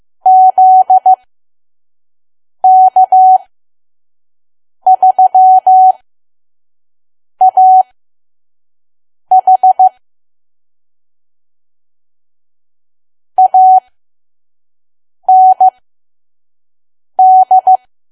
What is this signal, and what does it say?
Morse code, heard as a keyed tone, about 15 wpm, spaced as for 5 wpm, 750 Hz. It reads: ZK3AH AND
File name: Archive / 5 WPM files / PM